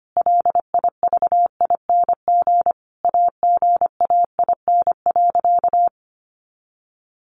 Morse code, heard as a keyed tone, about 25 wpm, 700 Hz. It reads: LIVING AGAIN.
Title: Sentences from Top 1500 Words 25wpm